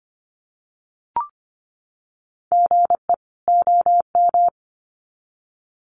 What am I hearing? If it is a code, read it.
Morse code, heard as a keyed tone, about 25 wpm, 700 Hz. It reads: GE OM
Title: Common QSO Phrases 25wpm